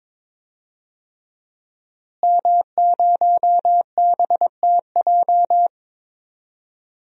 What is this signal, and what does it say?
Morse code, heard as a keyed tone, about 22 wpm, 700 Hz. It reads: M0BTJ
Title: DXCC-100 Call Signs - 5 Characters - Repeated 3x - 22wpm